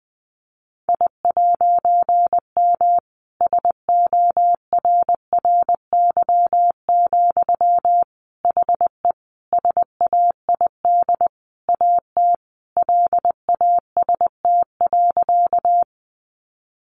Morse code, heard as a keyed tone, about 20 wpm, 700 Hz. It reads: I'M SORRY, HE SAID AT LAST.
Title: A Sound Of Thunder 20wpm